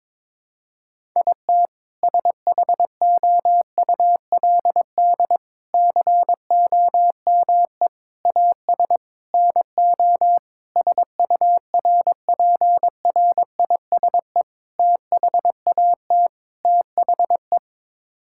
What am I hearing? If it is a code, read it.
Morse code, heard as a keyed tone, about 22 wpm, 700 Hz. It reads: IT SHOULD COME AS NO SURPRISE THAT THE
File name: practice / 22wpm / Sets of 8 Words - 22wpm